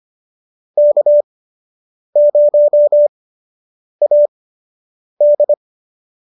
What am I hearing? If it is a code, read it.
Morse code, heard as a keyed tone, about 25 wpm, spaced as for 8 wpm, 590 Hz. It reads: K0AD